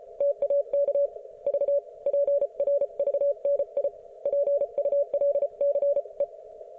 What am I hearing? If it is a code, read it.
Morse code, heard as a keyed tone, about 34 wpm, 570 Hz. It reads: TAK V PRVNI PULCE